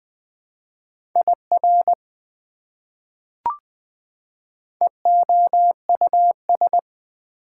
Morse code, heard as a keyed tone, about 20 wpm, 700 Hz. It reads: IR EOUS
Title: Prefixes and Suffixes 20wpm